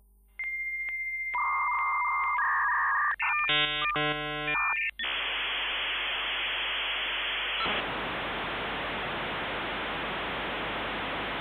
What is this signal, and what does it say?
Dial-up modem handshake.
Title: Modem 56k